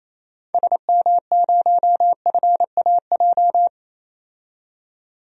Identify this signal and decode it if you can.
Morse code, heard as a keyed tone, about 28 wpm, 700 Hz. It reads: SM0FAJ
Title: DXCC-100 Call Signs - 6 Characters - 28wpm